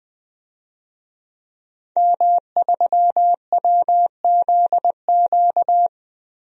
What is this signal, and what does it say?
Morse code, heard as a keyed tone, about 20 wpm, 700 Hz. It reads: M3WZQ
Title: DXCC-100 Call Signs - 5 Characters - Repeated 2x - 20wpm